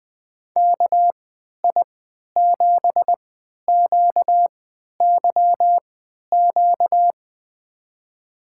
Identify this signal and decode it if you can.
Morse code, heard as a keyed tone, about 20 wpm, spaced as for 11 wpm, 700 Hz. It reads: KI7QYQ
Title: USA Call Signs - 6 Characters - 3x Character Spacing - 20wpm